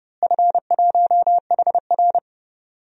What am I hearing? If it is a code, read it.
Morse code, heard as a keyed tone, about 32 wpm, 700 Hz. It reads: F1HR
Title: DXCC-100 Call Signs - 4 Characters - Repeated 3x - 30wpm